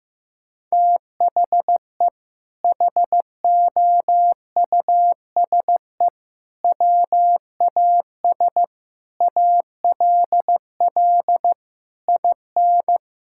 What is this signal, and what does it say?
Morse code, heard as a keyed tone, about 16 wpm, 700 Hz. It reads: THE HOUSE WAS ALL IN
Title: Sentences from Top 1600 Words 15wpm